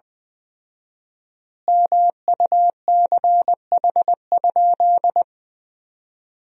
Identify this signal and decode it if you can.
Morse code, heard as a keyed tone, about 20 wpm, 700 Hz. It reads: MUCH?
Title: 4x Word-Spacing - Sentences from Top 200 Words - 20wpm